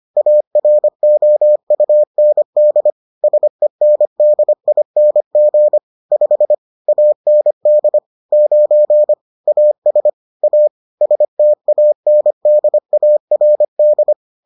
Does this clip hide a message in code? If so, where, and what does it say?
Yes: Morse code from 0.2 to 14.1 s, heard as a keyed tone, about 25 wpm, 600 Hz. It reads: AROUND SENDING 5 AND 9 AS A STANDARD